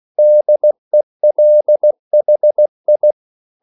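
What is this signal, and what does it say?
Morse code, heard as a keyed tone, about 16 wpm, 600 Hz. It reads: DELHI
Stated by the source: CW practice of national capitals at 16 words per minute, from ON6ZQ.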